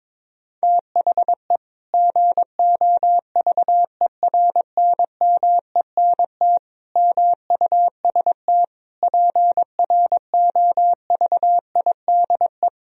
Morse code, heard as a keyed tone, about 22 wpm, 700 Hz. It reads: THE GOVERNMENT MUST PROVIDE
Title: Sentences from Top 1000 Words - Review Part 3 - 22wpm